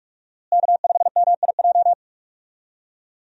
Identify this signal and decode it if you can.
Morse code, heard as a keyed tone, about 46 wpm, 700 Hz. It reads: K5MIJ
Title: USA Call Signs - 5 Characters - Repeated 3x - 45wpm